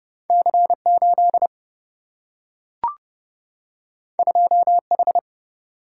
Morse code, heard as a keyed tone, about 30 wpm, 700 Hz. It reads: C8 2H
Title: Single-Letter Single-Number 30wpm